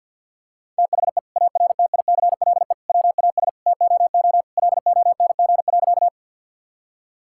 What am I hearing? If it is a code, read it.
Morse code, heard as a keyed tone, about 50 wpm, 700 Hz. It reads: THE ARTICLE WAS TOO LONG.
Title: Sentences from Top 1000 Words - Review Part 5 - 50wpm